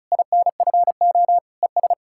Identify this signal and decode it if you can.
Morse code, heard as a keyed tone, about 48 wpm, spaced as for 42 wpm, 700 Hz. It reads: INFO ES
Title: Speed-Racing Common QSO Phrases Encore - 35wpm